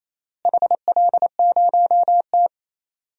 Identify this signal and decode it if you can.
Morse code, heard as a keyed tone, about 28 wpm, 700 Hz. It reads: HL0T
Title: DXCC-100 Call Signs - 4 Characters - Repeated 3x - 28wpm